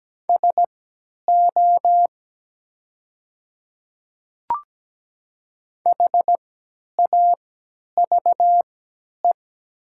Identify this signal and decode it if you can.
Morse code, heard as a keyed tone, about 17 wpm, spaced as for 10 wpm, 690 Hz. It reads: SO HAVE
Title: 3x Character Spacing - Top 100 Words - 17wpm